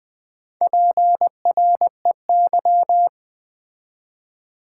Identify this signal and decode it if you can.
Morse code, heard as a keyed tone, about 20 wpm, 700 Hz. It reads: PREY
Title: Build A Word - 2 Letters At A Time - 4-Letter Words - 20wpm